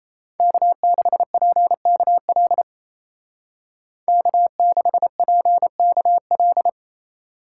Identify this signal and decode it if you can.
Morse code, heard as a keyed tone, about 36 wpm, 700 Hz. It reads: K6PKL K6PKL